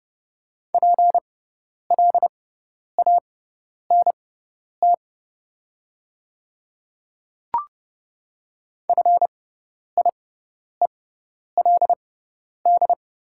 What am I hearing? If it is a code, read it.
Morse code, heard as a keyed tone, about 30 wpm, spaced as for 10 wpm, 690 Hz. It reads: PLANT FIELD
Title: 6x Character Spacing - Top 500 Words - 30wpm